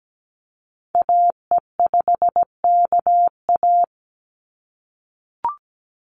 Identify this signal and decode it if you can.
Morse code, heard as a keyed tone, about 17 wpm, 700 Hz. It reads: AE5KA
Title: NA SOTA Activator Callsigns 17wpm